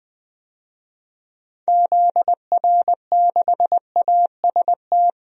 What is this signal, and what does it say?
Morse code, heard as a keyed tone, about 21 wpm, 700 Hz. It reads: ZR6AST
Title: DXCC-100 Call Signs - 6 Characters - 20wpm